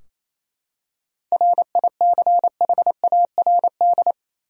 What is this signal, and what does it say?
Morse code, heard as a keyed tone, about 28 wpm, 700 Hz. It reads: RICHARD
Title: USA Common Names - 28wpm